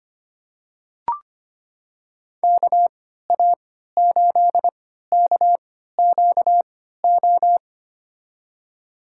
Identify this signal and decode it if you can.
Morse code, heard as a keyed tone, about 25 wpm, spaced as for 14 wpm, 700 Hz. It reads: KA8KQO
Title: USA Call Signs - 3x Character Spacing - 25wpm